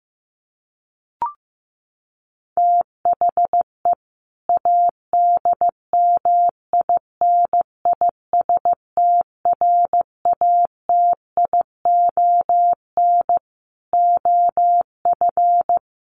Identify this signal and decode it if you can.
Morse code, heard as a keyed tone, about 14 wpm, 700 Hz. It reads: THE ADMINISTRATION OF
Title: Sentences from Top 1300 Words 15wpm